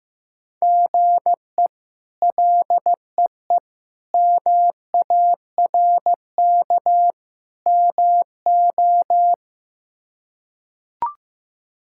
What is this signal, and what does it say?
Morse code, heard as a keyed tone, about 15 wpm, 700 Hz. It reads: GE LEE MARK MO
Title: K1USN SST QSO Practice - 15wpm